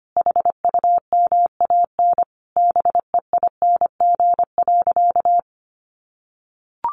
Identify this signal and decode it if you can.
Morse code, heard as a keyed tone, about 25 wpm, 700 Hz. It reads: HUMAN BEING.
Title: Sentences from Top 1500 Words 25wpm